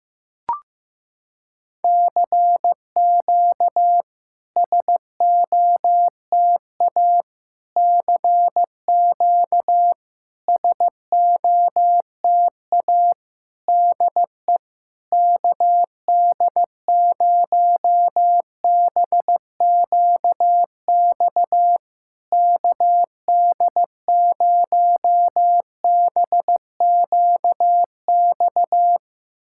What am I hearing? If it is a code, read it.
Morse code, heard as a keyed tone, about 15 wpm, 700 Hz. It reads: CQ SOTA CQ SOTA DE KD0BQX KD0BQX
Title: SOTA QSO Practice 15wpm